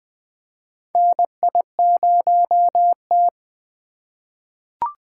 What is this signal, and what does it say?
Morse code, heard as a keyed tone, about 20 wpm, 700 Hz. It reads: NI0T
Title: USA Call Signs - 4x Character Spacing - 20wpm